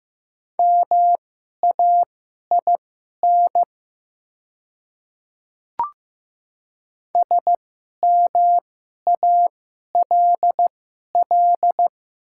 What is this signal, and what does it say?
Morse code, heard as a keyed tone, about 15 wpm, spaced as for 11 wpm, 700 Hz. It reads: MAIN SMALL
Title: 2x Character Spacing - Top 500 Words - 15wpm